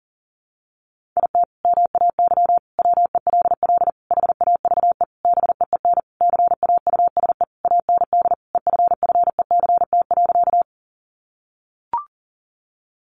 Morse code, heard as a keyed tone, about 40 wpm, 700 Hz. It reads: IT MAY WELL HAVE BEEN CAUSE AND EFFECT.
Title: Sentences from Top 1000 Words 40wpm